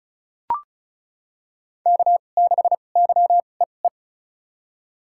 Morse code, heard as a keyed tone, about 34 wpm, spaced as for 25 wpm, 700 Hz. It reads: K6YEE
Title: USA Call Signs - 2x Character Spacing - 35wpm